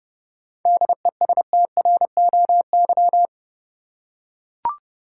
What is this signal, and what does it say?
Morse code, heard as a keyed tone, about 31 wpm, 700 Hz. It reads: DESTROY
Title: Seven Letter Words - 30wpm